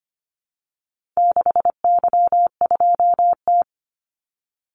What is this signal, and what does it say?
Morse code, heard as a keyed tone, about 25 wpm, 700 Hz. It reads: 6Y2T
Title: Call Signs Repeated 3x - 25wpm